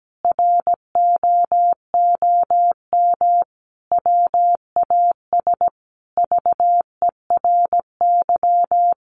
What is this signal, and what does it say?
Morse code, heard as a keyed tone, about 17 wpm, 700 Hz. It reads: ROOM WAS VERY
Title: Sentences from Top 1300 Words 17wpm